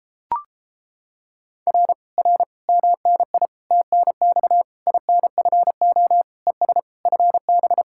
Morse code, heard as a keyed tone, about 29 wpm, 700 Hz. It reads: R R MNI TNX INFO ES FB